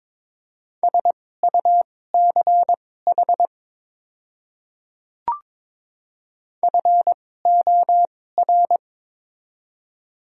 Morse code, heard as a keyed tone, about 22 wpm, spaced as for 16 wpm, 700 Hz. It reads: SUCH FOR
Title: 2x Character Spacing - Top 200 Words - 22wpm